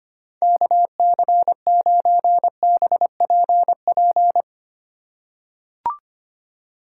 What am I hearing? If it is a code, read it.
Morse code, heard as a keyed tone, about 25 wpm, 700 Hz. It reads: KC9BPP
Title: USA Call Signs - 6 Characters - 3x Character Spacing - 25wpm